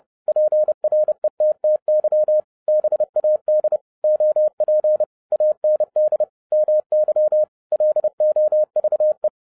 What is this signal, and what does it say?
Morse code, heard as a keyed tone, about 30 wpm, 610 Hz. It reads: PRETTY BAD OP AND MY LOVE